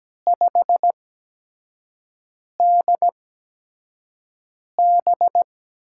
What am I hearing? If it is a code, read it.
Morse code, heard as a keyed tone, about 17 wpm, spaced as for 5 wpm, 700 Hz. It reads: 5DB
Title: USA Call Signs - 4 Characters - 8x Character Spacing - 17wpm